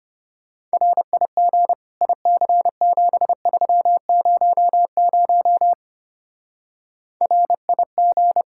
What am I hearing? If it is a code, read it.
Morse code, heard as a keyed tone, about 31 wpm, 700 Hz. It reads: RIG IC7300 RIG
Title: Speed-Racing Common QSO Phrases Encore - 25wpm